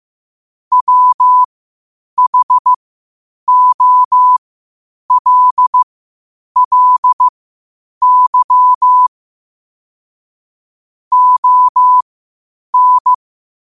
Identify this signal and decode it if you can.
Morse code, heard as a keyed tone, about 15 wpm, spaced as for 8 wpm, 1000 Hz. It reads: WHOLLY ON
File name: qotdfiles / 8wpm